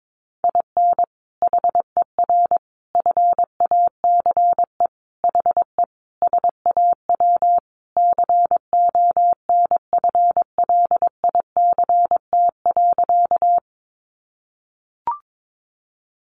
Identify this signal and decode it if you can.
Morse code, heard as a keyed tone, about 22 wpm, 700 Hz. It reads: IN HER FACE HE SAW CONFLICT.
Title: Sentences from Top 1300 Words 22wpm